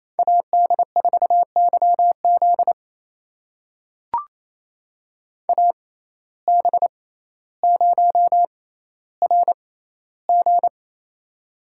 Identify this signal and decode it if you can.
Morse code, heard as a keyed tone, about 28 wpm, spaced as for 14 wpm, 700 Hz. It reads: AD4YZ AB0RG
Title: USA Call Signs - 5 Characters - 6x Character Spacing - 28wpm